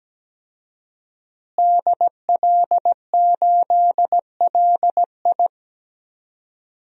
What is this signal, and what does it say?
Morse code, heard as a keyed tone, about 17 wpm, 700 Hz. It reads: DL8LI
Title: DXCC-100 Call Signs - 5 Characters - Repeated 3x - 17wpm